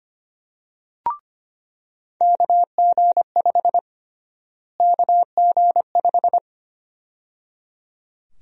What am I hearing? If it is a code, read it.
Morse code, heard as a keyed tone, about 25 wpm, 700 Hz. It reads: KG5 KG5
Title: USA Call Sign Prefixes - Repeated 2x - 25wpm